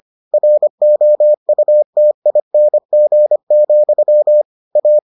Morse code, heard as a keyed tone, about 25 wpm, 600 Hz. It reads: ROUTING, A